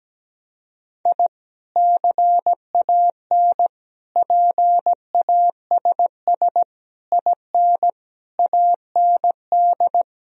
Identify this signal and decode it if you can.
Morse code, heard as a keyed tone, about 17 wpm, 700 Hz. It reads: I CAN PASS IN AND
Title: Sentences from Top 1000 Words - Review Part 4 - 17wpm